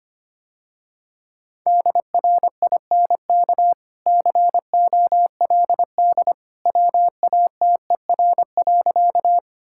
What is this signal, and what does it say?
Morse code, heard as a keyed tone, about 25 wpm, 700 Hz. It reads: DRINK COLD WATER.
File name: Easy Sentences with Hints - 25wpm